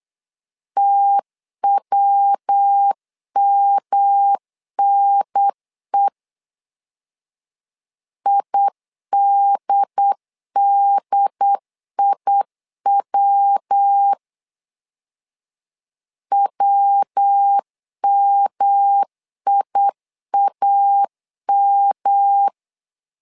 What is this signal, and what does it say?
Morse code, heard as a keyed tone, about 8 wpm, 800 Hz. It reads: TWMNE IDDIW WMIAM
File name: LEZIONI / 3 MWRO / 3 MW 8wpm 2501